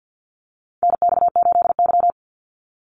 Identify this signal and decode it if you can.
Morse code, heard as a keyed tone, about 50 wpm, 700 Hz. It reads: NX8Y
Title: Call Signs - 50wpm